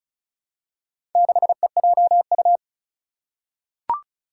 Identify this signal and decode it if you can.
Morse code, heard as a keyed tone, about 34 wpm, 700 Hz. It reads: 6EJU
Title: Four Challenging Letter-Number Combinations 35wpm